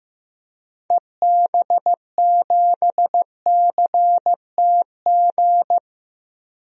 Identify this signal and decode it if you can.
Morse code, heard as a keyed tone, about 15 wpm, 700 Hz. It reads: EB7CTG
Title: DXCC-100 Call Signs - 6 Characters - 15wpm